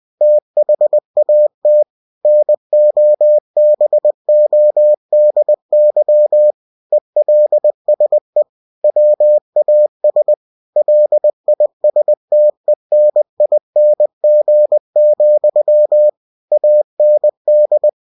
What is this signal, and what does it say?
Morse code, heard as a keyed tone, about 20 wpm, 600 Hz. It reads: THAT NOBODY ELSE WAS LISTENING, AND
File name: cw-books / winnie-the-pooh / winnie-20wpm-page-0034